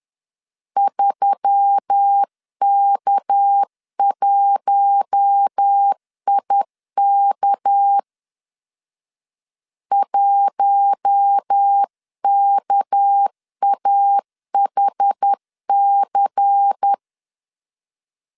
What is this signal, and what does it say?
Morse code, heard as a keyed tone, about 10 wpm, 800 Hz. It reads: 3K1IK 1KAHC
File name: LEZIONI / 6 PKX0 / 6 PK 10wpm 2436